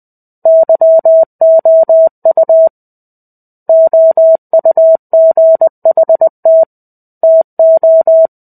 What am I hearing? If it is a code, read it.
Morse code, heard as a keyed tone, about 20 wpm, 650 Hz. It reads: YOU OUGHT TO
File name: 05-Chapter-05 20 WPM